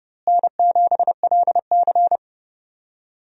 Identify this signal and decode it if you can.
Morse code, heard as a keyed tone, about 30 wpm, 700 Hz. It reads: N7LC
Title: USA Call Signs - 4 Characters - Repeated 2x - 30wpm